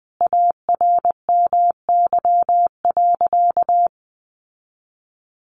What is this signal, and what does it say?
Morse code, heard as a keyed tone, about 20 wpm, 700 Hz. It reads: ARMY.
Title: Sentences from Top 1200 Words 20wpm